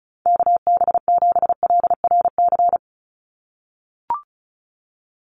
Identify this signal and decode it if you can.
Morse code, heard as a keyed tone, about 34 wpm, 700 Hz. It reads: KB7LRC
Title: USA Call Signs 35wpm